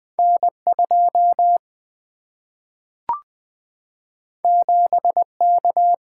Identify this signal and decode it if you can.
Morse code, heard as a keyed tone, about 20 wpm, 700 Hz. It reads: N 2 7 K
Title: Single-Letter Single-Number 20wpm